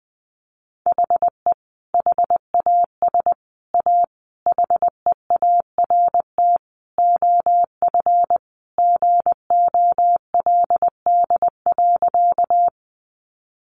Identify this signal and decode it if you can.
Morse code, heard as a keyed tone, about 20 wpm, 700 Hz. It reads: HE HAS A HEART OF GOLD.
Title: Sentences from Top 1200 Words 20wpm